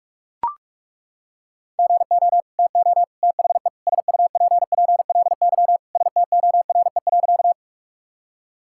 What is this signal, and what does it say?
Morse code, heard as a keyed tone, about 41 wpm, 700 Hz. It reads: GO TO THE SUPPLY STORE.
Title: Sentences from Top 1000 Words - Review Part 5 - 45wpm